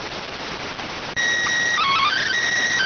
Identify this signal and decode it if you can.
SSTV (slow-scan television), PD 290.